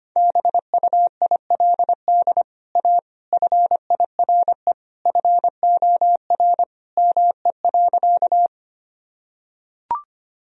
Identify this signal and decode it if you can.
Morse code, heard as a keyed tone, about 25 wpm, 700 Hz. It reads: BUILD A FIRE FOR ME.
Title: Sentences from Top 1000 Words - Review Part 1 - 25wpm